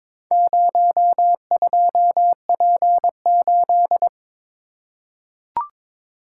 Morse code, heard as a keyed tone, about 23 wpm, 700 Hz. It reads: 02P8
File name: practice / 22wpm / Four Letter-Number Combinations - 22wpm